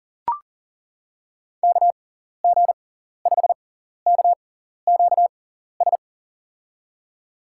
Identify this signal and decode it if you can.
Morse code, heard as a keyed tone, about 40 wpm, spaced as for 14 wpm, 700 Hz. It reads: KG5KQS
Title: USA Call Signs - 6x Character Spacing - 40wpm